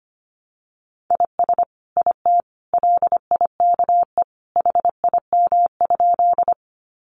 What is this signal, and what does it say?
Morse code, heard as a keyed tone, about 25 wpm, 700 Hz. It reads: IS IT LIKE HIM?
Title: Sentences from Top 100 Words 25wpm